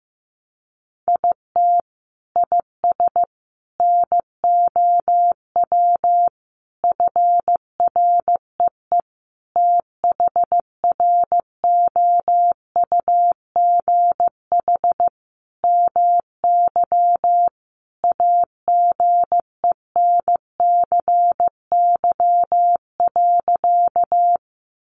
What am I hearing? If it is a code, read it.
Morse code, heard as a keyed tone, about 15 wpm, 700 Hz. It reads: IT IS NOW FREE THROUGH MY AGENCY.
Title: Sentences from Top 1300 Words 15wpm